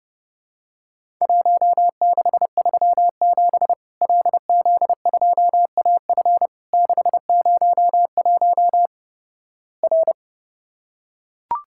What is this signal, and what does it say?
Morse code, heard as a keyed tone, about 30 wpm, 700 Hz. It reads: 1637 LZ2AF 601 R
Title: Worked All Europe DX Contest - QTC Exchange - 30wpm